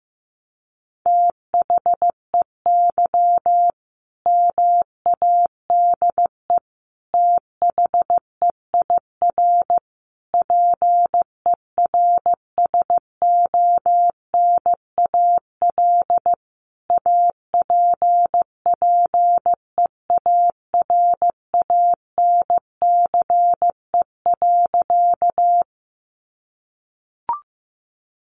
Morse code, heard as a keyed tone, about 15 wpm, 700 Hz. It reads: THEY MADE THEIR PERSONAL APPEARANCE.
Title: Sentences from Top 1100 Words 15wpm